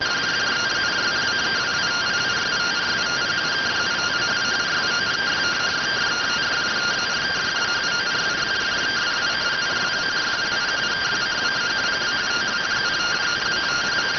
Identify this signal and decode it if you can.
Two-tone teleprinter signal, about 170 Hz shift.